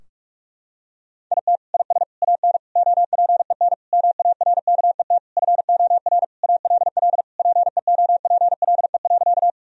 Morse code, heard as a keyed tone, about 41 wpm, 700 Hz. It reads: IT IS AN OPEN MARKET FOR ALL PEOPLE.
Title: Sentences from Top 1000 Words - Review Part 3 - 45wpm